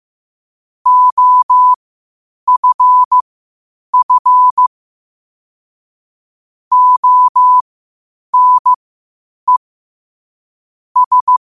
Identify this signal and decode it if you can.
Morse code, heard as a keyed tone, about 15 wpm, spaced as for 8 wpm, 1000 Hz. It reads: OFF ONE S
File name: qotdfiles / 8wpm